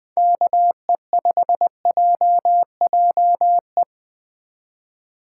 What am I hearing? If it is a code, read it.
Morse code, heard as a keyed tone, about 24 wpm, 700 Hz. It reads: KE5JJE
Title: Speed Racing - USA 2x3 Call Signs 20wpm